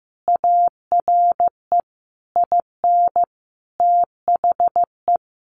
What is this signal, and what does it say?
Morse code, heard as a keyed tone, about 15 wpm, 700 Hz. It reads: ARE IN THE
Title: Sentences from Top 1300 Words 15wpm